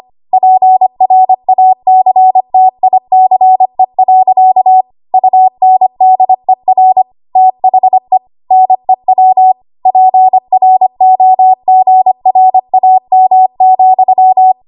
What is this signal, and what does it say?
Morse code, heard as a keyed tone, about 25 wpm, 750 Hz. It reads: PRACTICE. UNDER THE NEW PROGRAM,
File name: Archive / 25 WPM / 25WPM